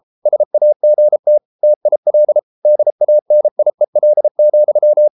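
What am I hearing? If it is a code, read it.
Morse code, heard as a keyed tone, about 33 wpm, 600 Hz. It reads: SAGT TIL DANIEL,
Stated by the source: Morse code at 33 wpm, QRQ.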